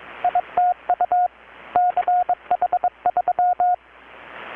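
Morse code, heard as a keyed tone, about 21 wpm, 720 Hz, with radio noise. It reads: ITU CH3
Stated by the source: Morse from KPH's Night of Nights XX maritime radiotelegraphy event, received in Chiapas, Mexico about 3,800 km away on a Kenwood R-600 with a grounded 10 m long-wire antenna.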